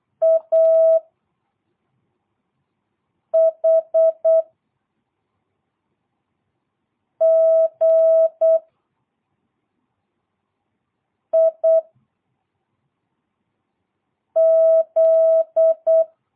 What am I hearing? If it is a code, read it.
Morse code, heard as a keyed tone, about 8 wpm, 660 Hz. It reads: A H G I Z
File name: jh8cbh / 2022cw